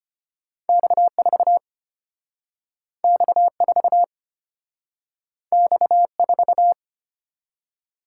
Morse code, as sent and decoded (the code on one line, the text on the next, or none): -..- ....- / -..- ....- / -..- ....-
X4 X4 X4